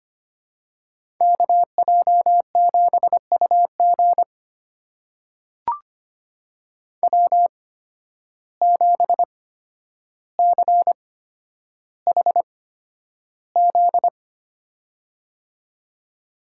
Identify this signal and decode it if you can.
Morse code, heard as a keyed tone, about 25 wpm, spaced as for 11 wpm, 700 Hz. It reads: KJ7UG W7CHZ
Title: USA Call Signs - 5 Characters - 8x Character Spacing - 25wpm